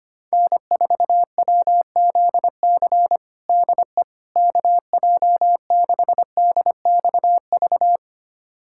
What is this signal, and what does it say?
Morse code, heard as a keyed tone, about 25 wpm, 700 Hz. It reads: N4WZC DE KJ6DXV
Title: USA Callsign DE Callsign 25wpm